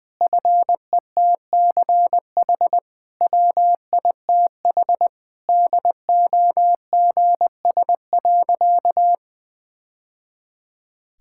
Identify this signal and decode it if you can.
Morse code, heard as a keyed tone, about 20 wpm, 700 Hz. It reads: FETCH WITH DOGS.
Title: Easy Sentences with Hints - 20wpm